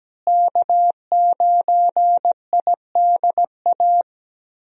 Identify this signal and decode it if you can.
Morse code, heard as a keyed tone, about 17 wpm, 700 Hz. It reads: K9IDA
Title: USA Call Signs - 5 Characters - Repeated 3x - 17wpm